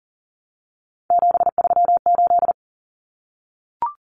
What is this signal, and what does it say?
Morse code, heard as a keyed tone, about 40 wpm, 700 Hz. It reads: 738
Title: Numbers - 3-Digit 40wpm